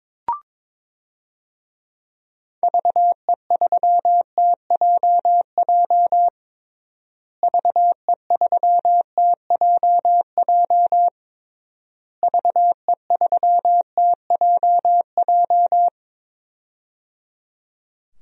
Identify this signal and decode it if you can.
Morse code, heard as a keyed tone, about 22 wpm, 700 Hz. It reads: VE3TJJ VE3TJJ VE3TJJ
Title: DXCC-100 Call Signs - 6 Characters - Repeated 3x - 22wpm